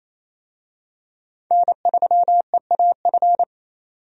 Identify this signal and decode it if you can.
Morse code, heard as a keyed tone, about 28 wpm, 700 Hz. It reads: N3EAF
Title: USA Call Signs - 5 Characters - 28wpm